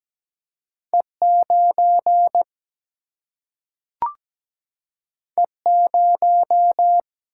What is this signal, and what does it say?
Morse code, heard as a keyed tone, about 17 wpm, 700 Hz. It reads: E 9 E 0
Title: Single-Letter Single-Number 17wpm